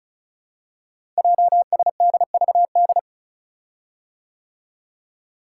Morse code, heard as a keyed tone, about 36 wpm, 700 Hz. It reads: JSDVD
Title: Code Groups - 5 Letters - 35wpm